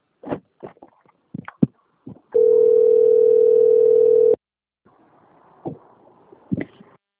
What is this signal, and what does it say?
Ringback tone (North America), 440 + 480 Hz.
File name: rg-1600-unknown- - - .76283